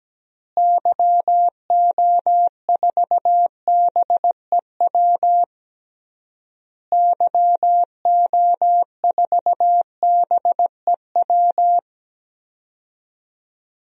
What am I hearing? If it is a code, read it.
Morse code, heard as a keyed tone, about 17 wpm, 700 Hz. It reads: YO4BEW YO4BEW
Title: DXCC-100 Call Signs - 6 Characters - Repeated 3x - 17wpm